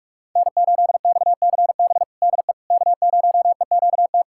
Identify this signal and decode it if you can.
Morse code, heard as a keyed tone, about 41 wpm, 700 Hz. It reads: N8XCB DE K0EQT
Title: USA Callsign DE Callsign - 5 Characters - 45wpm